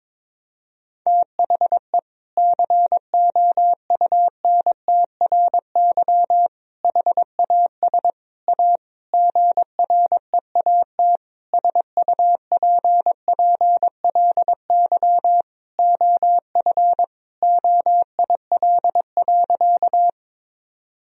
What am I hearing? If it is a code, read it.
Morse code, heard as a keyed tone, about 22 wpm, 700 Hz. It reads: THE COUNTRY HAS A GREAT SUPPLY OF OIL.
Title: Sentences from Top 1000 Words - Review Part 4 - 22wpm